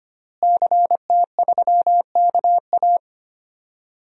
Morse code, heard as a keyed tone, about 25 wpm, 700 Hz. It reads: CT3KA
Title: DXCC-100 Call Signs - 5 Characters - Repeated 3x - 25wpm